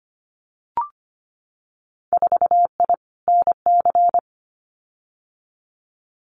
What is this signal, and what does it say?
Morse code, heard as a keyed tone, about 25 wpm, 700 Hz. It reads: 4I NC
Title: Winter Field Day Exchanges 25wpm